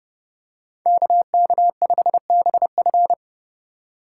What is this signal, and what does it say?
Morse code, heard as a keyed tone, about 30 wpm, 700 Hz. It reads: KK5BF
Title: USA Call Signs - 5 Characters - 4x Character Spacing - 30wpm